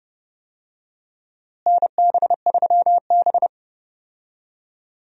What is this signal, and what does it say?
Morse code, heard as a keyed tone, about 30 wpm, 700 Hz. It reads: NB3B
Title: USA Call Signs - 4 Characters - Repeated 3x - 30wpm